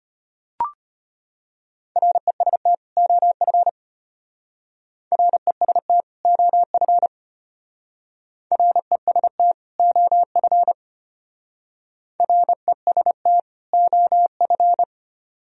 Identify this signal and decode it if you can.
Morse code, heard as a keyed tone, about 33 wpm, 700 Hz. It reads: REST OF REST OF REST OF REST OF